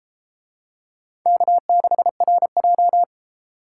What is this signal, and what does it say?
Morse code, heard as a keyed tone, about 32 wpm, 700 Hz. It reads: K6RJ